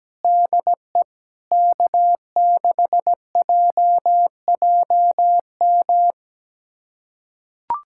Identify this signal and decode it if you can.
Morse code, heard as a keyed tone, about 17 wpm, 700 Hz. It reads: DE K6JJM
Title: USA Callsign DE Callsign - 5 Characters - 17wpm